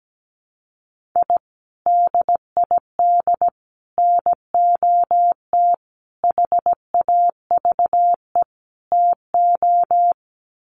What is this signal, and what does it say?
Morse code, heard as a keyed tone, about 17 wpm, 700 Hz. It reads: I DID NOT HAVE TO
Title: Sets of 5 Words 17wpm